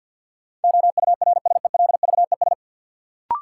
Morse code, heard as a keyed tone, about 53 wpm, 700 Hz. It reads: OURSELVES
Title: Sentences from Top 1600 Words - New Word Drill Alt - 50wpm